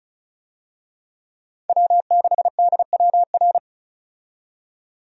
Morse code, heard as a keyed tone, about 36 wpm, 700 Hz. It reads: W6DWR
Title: USA Call Signs - 5 Characters - 35wpm